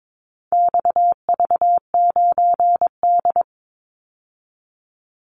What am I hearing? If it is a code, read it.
Morse code, heard as a keyed tone, about 22 wpm, 700 Hz. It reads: XV9D
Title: Call Signs - 22wpm